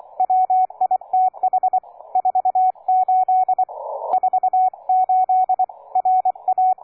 Morse code, heard as a keyed tone, about 24 wpm, 740 Hz, with radio noise. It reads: WITH 48 48 RA
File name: cw forum / K9DX 1818 le 17-12-2008 05h47